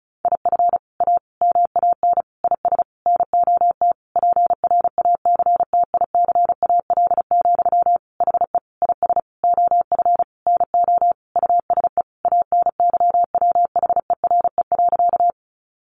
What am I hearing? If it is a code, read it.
Morse code, heard as a keyed tone, about 34 wpm, 700 Hz. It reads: IF A MAN IS NOT PRACTICAL, HE IS OF NO USE ANYWHERE.
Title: Sentences from Top 1400 Words 35wpm